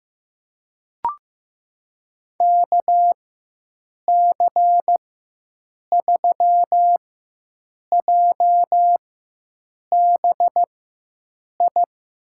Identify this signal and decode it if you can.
Morse code, heard as a keyed tone, about 15 wpm, spaced as for 7 wpm, 700 Hz. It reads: KC3JBI
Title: USA Call Signs - 6 Characters - 4x Character Spacing - 15wpm